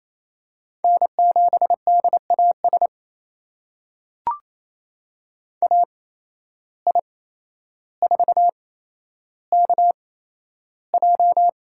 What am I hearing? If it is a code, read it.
Morse code, heard as a keyed tone, about 27 wpm, spaced as for 12 wpm, 700 Hz. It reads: N7DAS AI4KJ
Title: USA Call Signs - 5 Characters - 8x Character Spacing - 28wpm